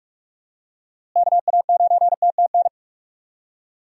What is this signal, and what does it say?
Morse code, heard as a keyed tone, about 46 wpm, 700 Hz. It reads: KA9TTN
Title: USA Call Signs - 6 Characters - Repeated 2x - 45wpm